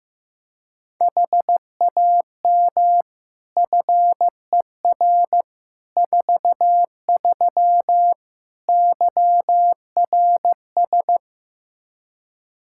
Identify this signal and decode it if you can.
Morse code, heard as a keyed tone, about 15 wpm, 700 Hz. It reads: HAM FER 43 YRS
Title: K1USN SST QSO Practice 15wpm